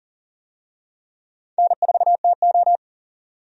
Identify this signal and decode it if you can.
Morse code, heard as a keyed tone, about 40 wpm, 700 Hz. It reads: N4TO
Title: USA Call Signs - 4 Characters - 40wpm